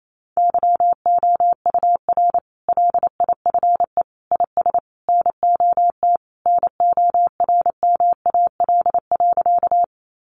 Morse code, heard as a keyed tone, about 28 wpm, 700 Hz. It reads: YOUR LIFE IS NOT NORMAL.
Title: Sentences from Top 1200 Words 28wpm